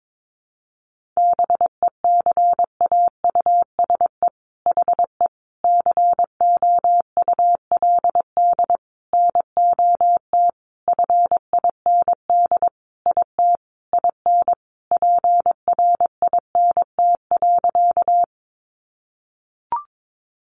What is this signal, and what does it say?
Morse code, heard as a keyed tone, about 22 wpm, 700 Hz. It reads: BECAUSE HE COULD NOT FIND IT IN PRINT.
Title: Sentences from Top 1400 Words 22wpm